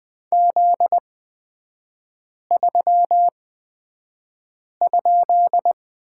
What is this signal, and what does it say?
Morse code, heard as a keyed tone, about 20 wpm, spaced as for 5 wpm, 700 Hz. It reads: Z3?
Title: 243 Lesson 39e - Letter z - Three Characters 20wpm